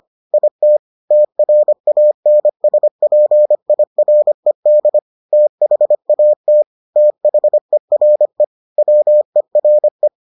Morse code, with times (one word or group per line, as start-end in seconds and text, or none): 0.3-0.8 IT
1.1-5.0 TRANSPIRED
5.3-6.6 THAT
7.0-8.4 THERE
8.8-10.1 WERE